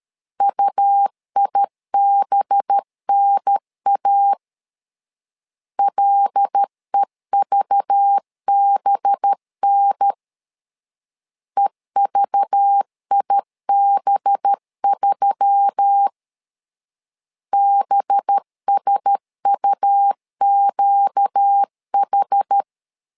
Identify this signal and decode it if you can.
Morse code, heard as a keyed tone, about 13 wpm, 800 Hz. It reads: UIBNA LEVBN EVIB3 BSUQH
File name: LEZIONI / 5 LBVC / 5 LBV 12wpm 1855